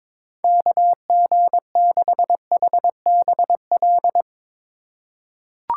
Morse code, heard as a keyed tone, about 22 wpm, 700 Hz. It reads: KG6HBL
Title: USA Call Signs - 6 Characters - 2x Character Spacing - 22wpm